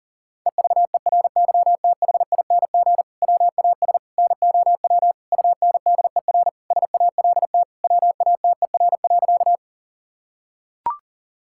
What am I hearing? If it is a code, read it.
Morse code, heard as a keyed tone, about 37 wpm, 700 Hz. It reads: EVERYTHING WAS NOW UNDER SALT WATER.
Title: Sentences from Top 1000 Words - Review Part 5 - 40wpm